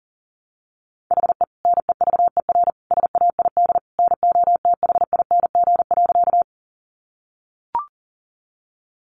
Morse code, heard as a keyed tone, about 40 wpm, 700 Hz. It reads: HE NEVER SAID NOTHING.
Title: Sentences from Top 500 Words 40wpm